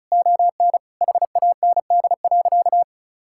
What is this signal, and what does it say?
Morse code, heard as a keyed tone, about 34 wpm, 700 Hz. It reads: ON HAND.
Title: Sentences from Top 1000 Words - Review Part 5 - 35wpm